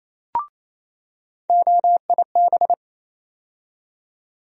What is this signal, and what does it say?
Morse code, heard as a keyed tone, about 28 wpm, 700 Hz. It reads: OIB
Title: USA Call Sign Postfixes - 3 Letters 28wpm